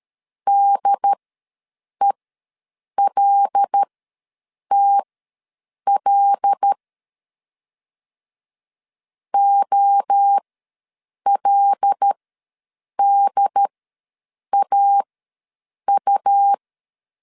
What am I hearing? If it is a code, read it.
Morse code, heard as a keyed tone, about 13 wpm, spaced as for 7 wpm, 800 Hz. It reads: DELTL OLDAU
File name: LEZIONI / 5 LBVC / 5 L ws 12wpm 2958